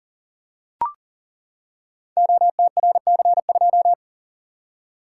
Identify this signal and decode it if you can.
Morse code, heard as a keyed tone, about 42 wpm, 700 Hz. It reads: OTRC2
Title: Code Groups - 5 Letter:Numbers - 40wpm